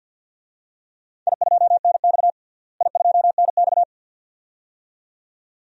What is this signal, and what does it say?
Morse code, heard as a keyed tone, about 52 wpm, 700 Hz. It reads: I2NX I2NX
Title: DXCC-100 Call Signs - 4 Characters - Repeated 2x - 50wpm